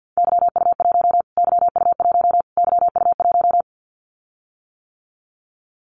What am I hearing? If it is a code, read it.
Morse code, heard as a keyed tone, about 50 wpm, 700 Hz. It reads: YU1 YU1 YU1
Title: Callsign Prefixes for Top 25 DXCC - Repeated 3x - 50wpm